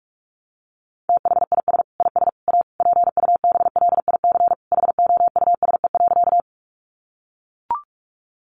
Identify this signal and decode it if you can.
Morse code, heard as a keyed tone, about 44 wpm, 700 Hz. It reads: THIS IS A PUBLIC HOUSE.
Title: Sentences from Top 600 Words 45wpm